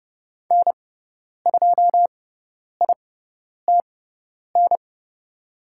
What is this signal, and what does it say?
Morse code, heard as a keyed tone, about 30 wpm, spaced as for 10 wpm, 700 Hz. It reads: N2ITN
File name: Lesson 17g - Letter u - Call Signs 30wpm